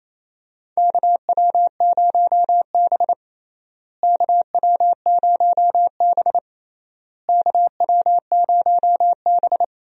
Morse code, heard as a keyed tone, about 28 wpm, 700 Hz. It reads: KW0B KW0B KW0B
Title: USA Call Signs - 4 Characters - Repeated 3x - 28wpm